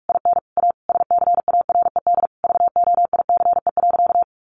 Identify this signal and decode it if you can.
Morse code, heard as a keyed tone, about 44 wpm, 700 Hz. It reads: IN A SCARED VOICE.
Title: Sentences from Top 700 Words 45wpm